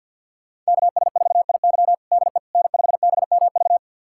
Morse code, heard as a keyed tone, about 52 wpm, 700 Hz. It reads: KS4IY DE N5BMV
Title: USA Callsign DE Callsign - 5 Characters - 50wpm